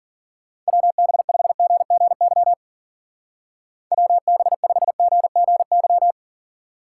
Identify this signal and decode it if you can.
Morse code, heard as a keyed tone, about 50 wpm, 700 Hz. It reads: WB5GGY WB5GGY
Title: Speed Racing - USA 2x3 Call Signs 40wpm